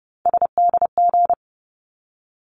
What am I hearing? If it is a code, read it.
Morse code, heard as a keyed tone, about 30 wpm, 700 Hz. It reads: SDG
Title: ARRL Sections 30wpm